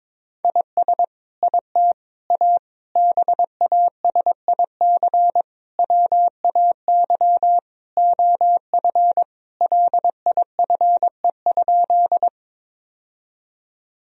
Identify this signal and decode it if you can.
Morse code, heard as a keyed tone, about 21 wpm, 700 Hz. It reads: IS IT A BASIC WAY OF LIFE?
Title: Sentences from Top 1000 Words - Review Part 3 - 22wpm